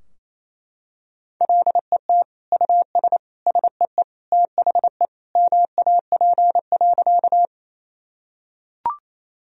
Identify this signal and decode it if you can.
Morse code, heard as a keyed tone, about 28 wpm, 700 Hz. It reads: LET US SEE THE MAP.
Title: Sentences from Top 1000 Words - Review Part 2 - 28wpm